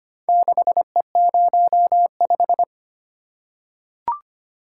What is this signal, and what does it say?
Morse code, heard as a keyed tone, about 25 wpm, 700 Hz. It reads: 6E05
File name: practice / 25wpm / Four Letter-Number Combinations - 25wpm